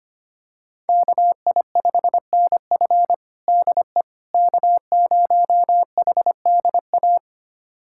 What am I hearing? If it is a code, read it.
Morse code, heard as a keyed tone, about 25 wpm, 700 Hz. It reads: KI5NF DE K0HDA
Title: USA Callsign DE Callsign - 5 Characters - 25wpm